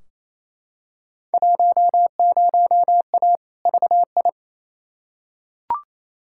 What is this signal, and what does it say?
Morse code, heard as a keyed tone, about 28 wpm, 700 Hz. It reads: 10A VI
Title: Field Day Exchanges Encore - 28wpm